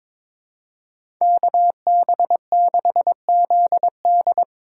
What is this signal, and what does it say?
Morse code, heard as a keyed tone, about 22 wpm, 700 Hz. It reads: KB6ZD
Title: USA Call Signs - 5 Characters - 4x Character Spacing - 22wpm